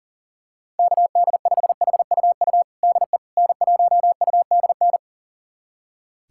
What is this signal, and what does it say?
Morse code, heard as a keyed tone, about 37 wpm, 700 Hz. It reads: KD5HUU DE N1UDN
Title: USA Callsign DE Callsign 40wpm